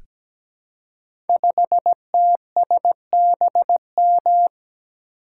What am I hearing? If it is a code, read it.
Morse code, heard as a keyed tone, about 17 wpm, 700 Hz. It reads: 5TSBM
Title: Code Groups - 5 Letter:Numbers - 17wpm